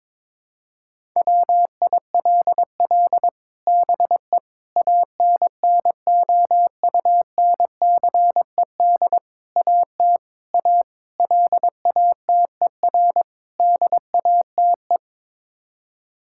Morse code, heard as a keyed tone, about 22 wpm, 700 Hz. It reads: WILL BE ANNOUNCED AT A LATER DATE
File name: practice / 22wpm / Sets of 7 Words - 22wpm